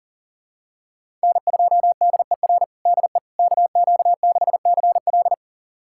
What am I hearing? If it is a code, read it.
Morse code, heard as a keyed tone, about 40 wpm, 700 Hz. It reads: N2DER DE KQ6CL
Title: USA Callsign DE Callsign - 5 Characters - 40wpm